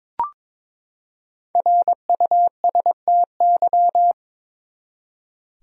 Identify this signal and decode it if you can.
Morse code, heard as a keyed tone, about 22 wpm, 700 Hz. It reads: RUSTY
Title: USA Common Names - Max 5 Letters - 22wpm